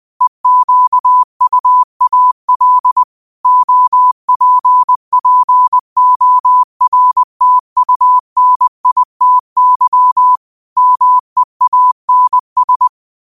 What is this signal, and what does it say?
Morse code, heard as a keyed tone, about 20 wpm, 1000 Hz. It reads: EQUAL OPPORTUNITY MEANS
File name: qotdfiles / 20wpm